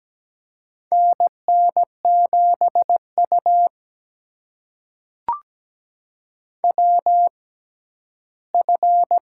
Morse code, heard as a keyed tone, about 17 wpm, spaced as for 11 wpm, 700 Hz. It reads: NN7U WF
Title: USA Call Signs - 4 Characters - 6x Character Spacing - 17wpm